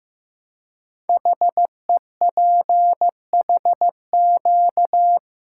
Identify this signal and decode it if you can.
Morse code, heard as a keyed tone, about 15 wpm, 700 Hz. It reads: HEPHQ
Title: Code Groups - 5 Letters - 15wpm